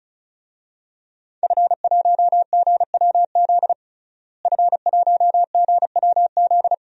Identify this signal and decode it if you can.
Morse code, heard as a keyed tone, about 36 wpm, 700 Hz. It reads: F1GWZ F1GWZ
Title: DXCC-100 Call Signs - 5 Characters - Repeated 2x - 35wpm